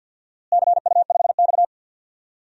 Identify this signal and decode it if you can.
Morse code, heard as a keyed tone, about 52 wpm, 700 Hz. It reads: CU5X
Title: DXCC-100 Call Signs - 4 Characters - Repeated 2x - 50wpm